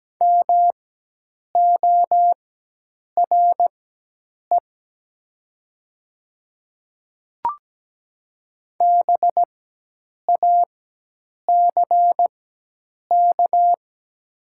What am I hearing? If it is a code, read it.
Morse code, heard as a keyed tone, about 17 wpm, spaced as for 8 wpm, 700 Hz. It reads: MORE BACK
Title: 4x Character Spacing - Top 200 Words - 17wpm